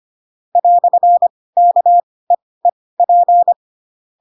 Morse code, heard as a keyed tone, about 25 wpm, spaced as for 18 wpm, 700 Hz. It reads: "KEEP
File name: Pinocchio / Pinocchio ch4 12 25 ik1hge